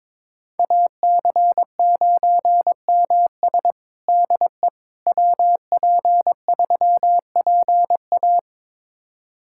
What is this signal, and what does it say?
Morse code, heard as a keyed tone, about 22 wpm, 700 Hz. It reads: AC9MS DE WP3PA
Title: USA Callsign DE Callsign - 5 Characters - 22wpm